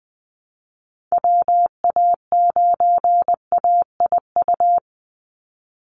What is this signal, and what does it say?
Morse code, heard as a keyed tone, about 20 wpm, 700 Hz. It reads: WA9AIU
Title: USA Call Signs 20wpm